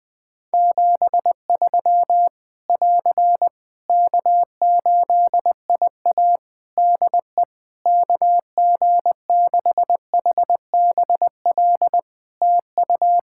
Morse code, heard as a keyed tone, about 20 wpm, 700 Hz. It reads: 73 <AR> K8IA DE KG6HBL TU
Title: Practice QSO - Exchange 4 - Ending - 20wpm